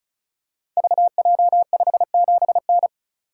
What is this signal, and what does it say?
Morse code, heard as a keyed tone, about 36 wpm, 700 Hz. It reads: VJ57N
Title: Code Groups - 5 Letter:Numbers - 35wpm